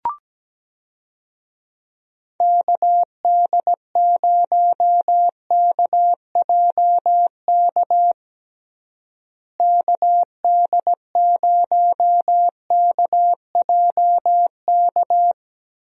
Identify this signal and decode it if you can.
Morse code, heard as a keyed tone, about 17 wpm, 700 Hz. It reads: KD0KJK KD0KJK
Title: USA Call Signs - 6 Characters - Repeated 3x - 17wpm